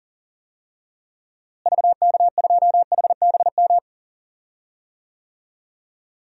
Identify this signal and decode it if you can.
Morse code, heard as a keyed tone, about 40 wpm, 700 Hz. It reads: VK2HBM
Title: DXCC-100 Call Signs - 6 Characters - Repeated 3x - 40wpm